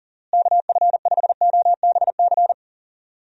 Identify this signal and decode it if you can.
Morse code, heard as a keyed tone, about 38 wpm, 700 Hz. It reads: KF5OBC
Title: USA Call Signs - 6 Characters - Repeated 2x - 40wpm